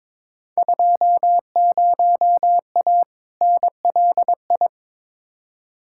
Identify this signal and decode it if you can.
Morse code, heard as a keyed tone, about 22 wpm, 700 Hz. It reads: 20A NLI
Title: Field Day Exchanges Encore - 22wpm